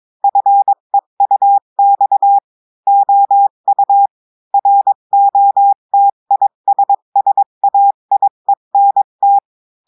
Morse code, heard as a keyed tone, about 22 wpm, 800 Hz. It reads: FEUX OU ROTISSAIENT
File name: cours-du-mercredi / audios / cours-853-du-2-mai-2024-etxte chiffre et texte clair-a-20-wpm